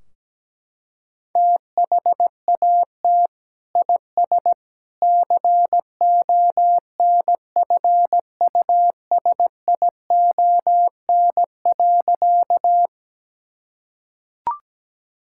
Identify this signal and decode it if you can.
Morse code, heard as a keyed tone, about 17 wpm, 700 Hz. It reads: THAT IS CONFUSION.
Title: Sentences from Top 1600 Words 17wpm